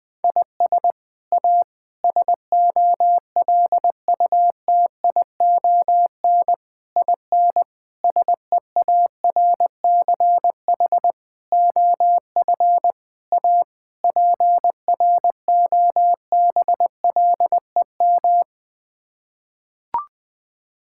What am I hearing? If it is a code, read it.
Morse code, heard as a keyed tone, about 21 wpm, 700 Hz. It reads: IS A SOLUTION IN SEARCH OF A PROBLEM
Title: Sets of 8 Words - 20wpm